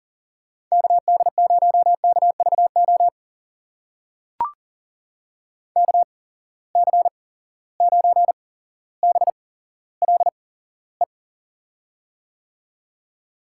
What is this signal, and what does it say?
Morse code, heard as a keyed tone, about 39 wpm, spaced as for 17 wpm, 700 Hz. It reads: KD0KVO KC9BLE
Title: USA Call Signs - 8x Character Spacing - 40wpm